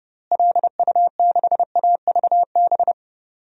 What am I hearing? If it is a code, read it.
Morse code, heard as a keyed tone, about 30 wpm, 700 Hz. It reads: LU6AVB
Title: DXCC-100 Call Signs - 6 Characters - Repeated 3x - 30wpm